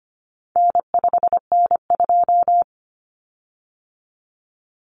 Morse code, heard as a keyed tone, about 25 wpm, 700 Hz. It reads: N5N2
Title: Common Cut Numbers - 4-Digit 25wpm